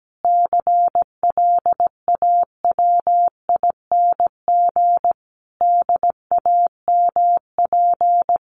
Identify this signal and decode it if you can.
Morse code, heard as a keyed tone, about 17 wpm, 700 Hz. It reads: CLAWING DAMP
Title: A Sound Of Thunder 17wpm